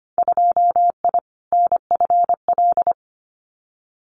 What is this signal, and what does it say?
Morse code, heard as a keyed tone, about 25 wpm, 700 Hz. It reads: 2I NFL
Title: Winter Field Day Exchanges 25wpm